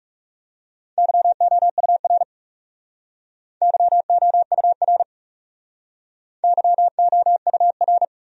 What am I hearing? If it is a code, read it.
Morse code, heard as a keyed tone, about 38 wpm, 700 Hz. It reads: YOUR YOUR YOUR